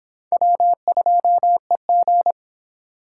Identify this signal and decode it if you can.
Morse code, heard as a keyed tone, about 25 wpm, 700 Hz. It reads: W2EG